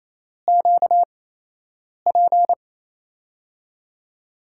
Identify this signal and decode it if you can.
Morse code, heard as a keyed tone, about 28 wpm, spaced as for 8 wpm, 700 Hz. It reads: QP
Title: USA Call Signs - 5 Characters - 8x Character Spacing - 28wpm